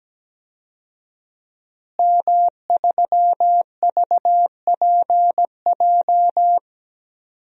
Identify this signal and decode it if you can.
Morse code, heard as a keyed tone, about 17 wpm, 700 Hz. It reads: M3VPJ
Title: DXCC-100 Call Signs - 5 Characters - Repeated 2x - 17wpm